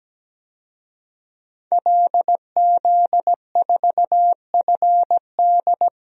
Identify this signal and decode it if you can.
Morse code, heard as a keyed tone, about 17 wpm, 700 Hz. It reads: LZ4FD
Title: DXCC-100 Call Signs - 5 Characters - Repeated 3x - 17wpm